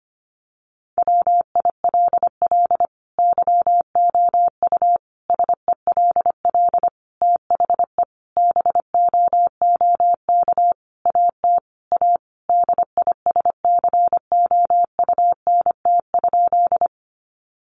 Morse code, heard as a keyed tone, about 25 wpm, 700 Hz. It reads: WILL YOU SELL THE BOOK AT A DISCOUNT?
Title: Sentences from Top 1200 Words 25wpm